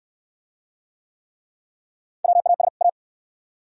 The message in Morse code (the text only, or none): WAS A